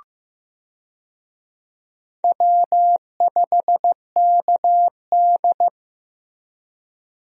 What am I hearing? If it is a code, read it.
Morse code, heard as a keyed tone, about 15 wpm, 700 Hz. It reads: W5KD
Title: USA Call Signs - 4 Characters - Repeated 3x - 15wpm